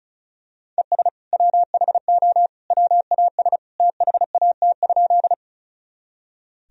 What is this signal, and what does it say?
Morse code, heard as a keyed tone, about 34 wpm, 700 Hz. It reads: ES WHO WAS THAT?
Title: Sentences from Top 100 Words and Abbreviations 35wpm